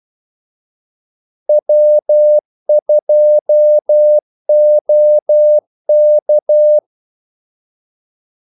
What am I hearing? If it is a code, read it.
Morse code, heard as a keyed tone, about 12 wpm, 600 Hz. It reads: W2OK
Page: 2 ekstra ordmellemrum mellem kaldesignalerne